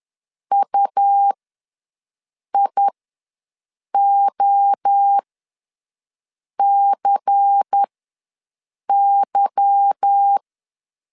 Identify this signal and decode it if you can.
Morse code, heard as a keyed tone, about 10 wpm, spaced as for 5 wpm, 800 Hz. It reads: UIOCY
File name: LEZIONI / 8 YJ45 / 8 YJ4 ws 10wpm 4020